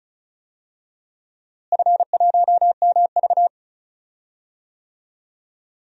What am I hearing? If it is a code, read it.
Morse code, heard as a keyed tone, about 34 wpm, 700 Hz. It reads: F1MV
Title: DXCC-100 Call Signs - 4 Characters - Repeated 2x - 35wpm